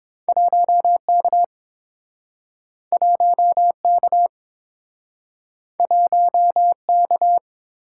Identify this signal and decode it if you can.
Morse code, heard as a keyed tone, about 30 wpm, 700 Hz. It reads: 1K 1K 1K